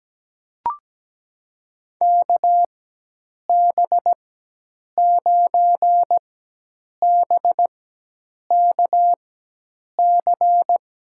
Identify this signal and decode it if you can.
Morse code, heard as a keyed tone, about 17 wpm, spaced as for 8 wpm, 700 Hz. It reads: KB9BKC
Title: USA Call Signs - 6 Characters - 4x Character Spacing - 17wpm